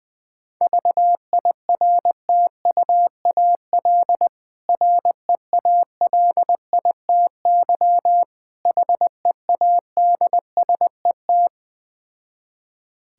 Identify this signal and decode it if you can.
Morse code, heard as a keyed tone, about 21 wpm, 690 Hz. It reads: VIRTUAL REALITY HEADSET
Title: Difficult Words with Hints - 20wpm